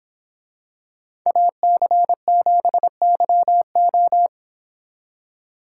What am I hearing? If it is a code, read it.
Morse code, heard as a keyed tone, about 27 wpm, 700 Hz. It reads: AC7YO
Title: Speed Racing - USA 2x2 Call Signs 22wpm